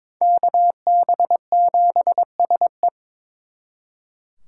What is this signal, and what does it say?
Morse code, heard as a keyed tone, about 26 wpm, 700 Hz. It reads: KB7SE
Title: Speed Racing - USA 2x2 Call Signs 22wpm